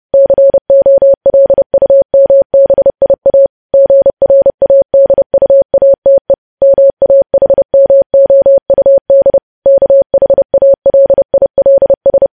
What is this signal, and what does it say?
Morse code, heard as a keyed tone, about 30 wpm, 550 Hz. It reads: COLUMBIA GRADUATE MAHMOUD KHALILS
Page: news headlines in morse code at 30 wpm